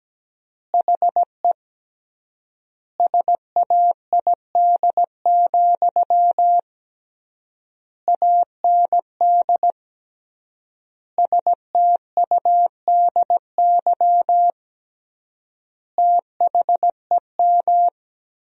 Morse code, heard as a keyed tone, about 17 wpm, 700 Hz. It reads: HE SAID, AND STUDY THEM
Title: 3x Word-Spacing - Sentences from Top 300 Words - 17wpm